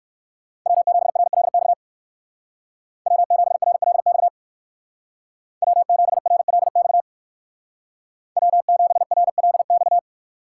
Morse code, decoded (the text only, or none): W7RLX W7RLX W7RLX W7RLX